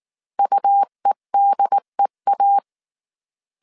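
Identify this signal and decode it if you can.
Morse code, heard as a keyed tone, about 19 wpm, 800 Hz. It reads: UEDEA
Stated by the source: Morse code dictation at 18 wpm from the IW3SGT QRS Morse code training course, five-character groups of 300 characters.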